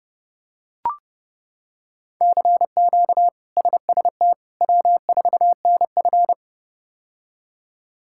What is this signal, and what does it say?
Morse code, heard as a keyed tone, about 30 wpm, 700 Hz. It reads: CQ SST W4NF
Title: K1USN SST QSO Practice - 30wpm